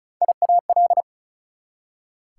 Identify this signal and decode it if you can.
Morse code, heard as a keyed tone, about 48 wpm, spaced as for 42 wpm, 700 Hz. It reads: IAL
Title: Speed-Racing Prefixes and Suffixes 35wpm